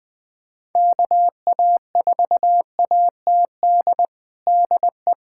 Morse code, heard as a keyed tone, about 21 wpm, 700 Hz. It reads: KA4ATD DE
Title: USA Callsign DE Callsign - 6 Characters - 20wpm